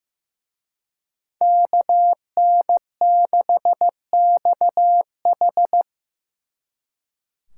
Morse code, heard as a keyed tone, about 18 wpm, 700 Hz. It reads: KN6XH
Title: Speed Racing - USA Call Signs 15wpm